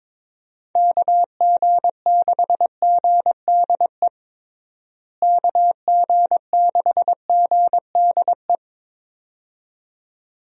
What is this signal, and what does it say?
Morse code, heard as a keyed tone, about 22 wpm, 700 Hz. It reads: KG6GDE KG6GDE
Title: USA Call Signs - 6 Characters - Repeated 2x - 22wpm